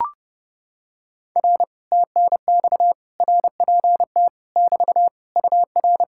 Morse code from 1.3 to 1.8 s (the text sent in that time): R